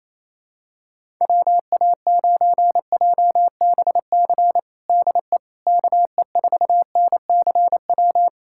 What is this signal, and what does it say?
Morse code, heard as a keyed tone, about 28 wpm, 700 Hz. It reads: WA9JBC DE KE4NCW
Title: USA Callsign DE Callsign - 6 Characters - 28wpm